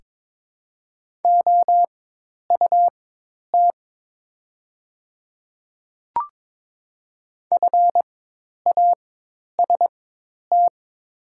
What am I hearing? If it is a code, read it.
Morse code, heard as a keyed tone, about 22 wpm, spaced as for 10 wpm, 700 Hz. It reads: OUT FAST
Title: 4x Character Spacing - Top 500 Words - 22wpm